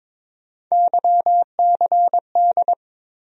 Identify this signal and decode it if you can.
Morse code, heard as a keyed tone, about 22 wpm, 700 Hz. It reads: YCD
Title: USA Call Sign Postfixes - 3 Letters 22wpm